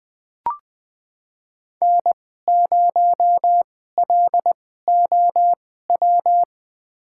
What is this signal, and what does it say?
Morse code, heard as a keyed tone, about 20 wpm, spaced as for 15 wpm, 700 Hz. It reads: N0LOW
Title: USA Call Signs - 5 Characters - 2x Character Spacing - 20wpm